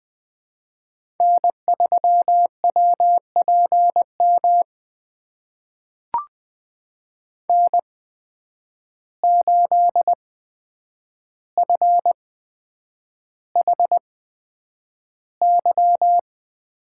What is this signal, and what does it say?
Morse code, heard as a keyed tone, about 20 wpm, spaced as for 9 wpm, 700 Hz. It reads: N3WPM N8FHY
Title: USA Call Signs - 5 Characters - 8x Character Spacing - 20wpm